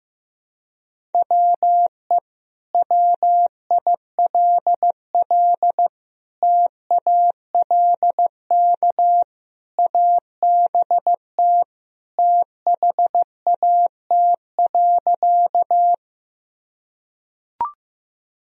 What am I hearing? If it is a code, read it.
Morse code, heard as a keyed tone, about 15 wpm, 700 Hz. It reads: WE WILL TALK ABT THAT.
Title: Sentences from Top 400 Words and Abbreviations 15wpm